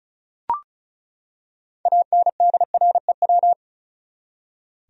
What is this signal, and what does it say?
Morse code, heard as a keyed tone, about 34 wpm, 700 Hz. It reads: ANDREW
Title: USA Common Names - 35wpm